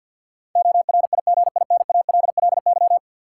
Morse code, heard as a keyed tone, about 52 wpm, 700 Hz. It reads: ORIGINALLY
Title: Sentences from Top 1500 Words - New Word Drill - 50wpm